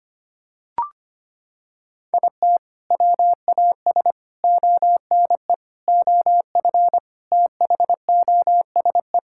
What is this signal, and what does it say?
Morse code, heard as a keyed tone, about 25 wpm, 700 Hz. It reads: IT WAS ONE OF THOSE
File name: practice / 25wpm / Sentences with 4-Word Phrases - 25wpm